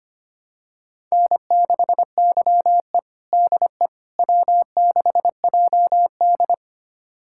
Morse code, heard as a keyed tone, about 25 wpm, 700 Hz. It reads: N6YE DE W6JD
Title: USA Callsign DE Callsign - 4 Characters - 25wpm